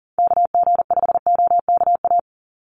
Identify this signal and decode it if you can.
Morse code, heard as a keyed tone, about 39 wpm, 700 Hz. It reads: KG5OKA
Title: USA Call Signs 40wpm